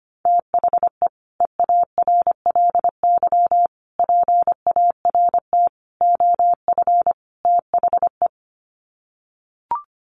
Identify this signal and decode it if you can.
Morse code, heard as a keyed tone, about 25 wpm, 700 Hz. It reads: THE EARLY PART OF THE
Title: Sets of 5 Words Encore2 25wpm